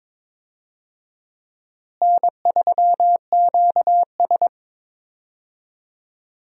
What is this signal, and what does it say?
Morse code, heard as a keyed tone, about 22 wpm, 700 Hz. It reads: N3QS
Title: USA Call Signs - 4 Characters - Repeated 3x - 22wpm